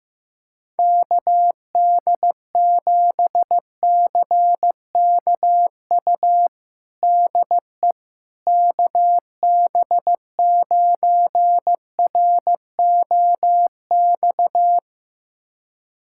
Morse code, heard as a keyed tone, about 15 wpm, 700 Hz. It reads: KD7CKU DE KB9ROX
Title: USA Callsign DE Callsign 15wpm